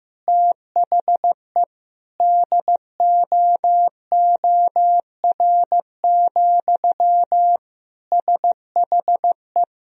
Morse code, heard as a keyed tone, about 15 wpm, 700 Hz. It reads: THE DOOR, SHE
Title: Sentences from Top 1000 Words - Review Part 1 - 15wpm